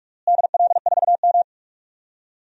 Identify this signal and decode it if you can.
Morse code, heard as a keyed tone, about 46 wpm, 700 Hz. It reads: DL4M
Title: DXCC-100 Call Signs - 4 Characters - Repeated 2x - 45wpm